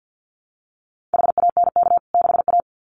Morse code, heard as a keyed tone, about 50 wpm, 700 Hz. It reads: HANK 6A
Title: CWT Exchange 50wpm